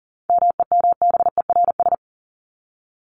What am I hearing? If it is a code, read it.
Morse code, heard as a keyed tone, about 40 wpm, 700 Hz. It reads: MEMBERS
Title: ES:S Suffix - Top 100 Words with Base Words 40wpm